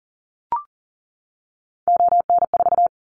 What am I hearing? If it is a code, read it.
Morse code, heard as a keyed tone, about 40 wpm, 700 Hz. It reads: ON4
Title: Callsign Prefixes for Top 25 DXCC - 40wpm